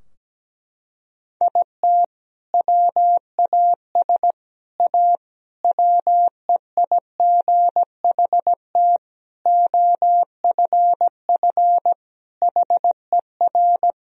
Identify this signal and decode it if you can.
Morse code, heard as a keyed tone, about 17 wpm, 700 Hz. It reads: IT WAS A WEIGHT OFF HER
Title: Sentences from Top 1000 Words - Review Part 3 - 17wpm